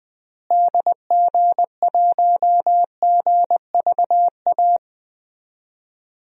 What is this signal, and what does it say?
Morse code, heard as a keyed tone, about 21 wpm, 700 Hz. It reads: DG1GVA
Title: DXCC-100 Call Signs - 6 Characters - Repeated 2x - 20wpm